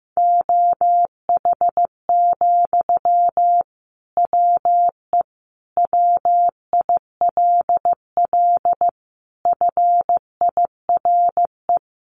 Morse code, heard as a keyed tone, about 15 wpm, 700 Hz. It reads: OH, WE WILL FIRE
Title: Sentences from Top 1100 Words 15wpm